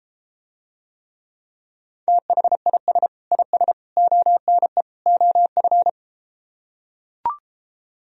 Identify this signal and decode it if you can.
Morse code, heard as a keyed tone, about 33 wpm, 700 Hz. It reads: THIS IS ONE OF